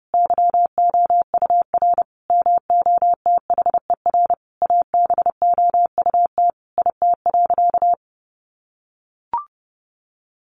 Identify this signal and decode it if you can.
Morse code, heard as a keyed tone, about 30 wpm, 700 Hz. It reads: YOUR MOTHER ABOUT IT.
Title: Sentences from Top 1100 Words 30wpm